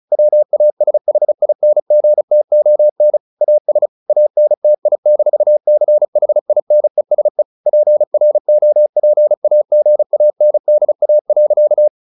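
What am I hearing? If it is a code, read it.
Morse code, heard as a keyed tone, about 34 wpm, 600 Hz. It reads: WASHINGTON AS ANTI-CHINESE PROPAGANDA.
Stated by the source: Morse practice at 35 wpm from CW med Gnister, sending Ken Follett's novel Never in English.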